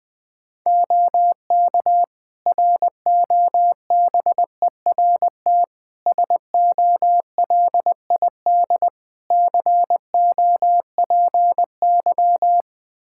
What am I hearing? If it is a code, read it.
Morse code, heard as a keyed tone, about 20 wpm, 700 Hz. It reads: OK ROBERT SOLID COPY
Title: K1USN SST QSO Practice 20wpm